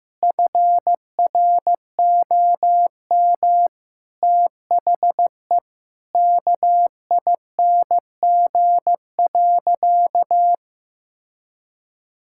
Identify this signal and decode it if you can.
Morse code, heard as a keyed tone, about 15 wpm, 700 Hz. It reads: FROM THE KING.
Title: Sentences from Top 1000 Words - Review Part 5 - 15wpm